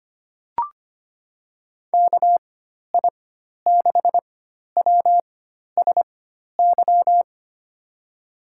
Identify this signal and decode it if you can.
Morse code, heard as a keyed tone, about 25 wpm, spaced as for 12 wpm, 700 Hz. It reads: KI6WSY
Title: USA Call Signs - 6 Characters - 4x Character Spacing - 25wpm